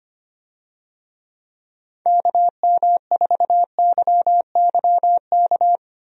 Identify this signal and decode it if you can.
Morse code, heard as a keyed tone, about 25 wpm, 700 Hz. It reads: KM4YYK
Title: USA Call Signs - 6 Characters - Repeated 2x - 25wpm